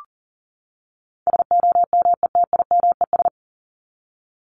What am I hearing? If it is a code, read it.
Morse code, heard as a keyed tone, about 40 wpm, 700 Hz. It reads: SOMETIMES
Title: Top 501-750 Words 40wpm